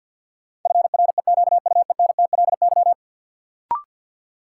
Morse code, heard as a keyed tone, about 53 wpm, 700 Hz. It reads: FREQUENTLY
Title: Sentences from Top 1300 Words - New Word Drill Alt - 50wpm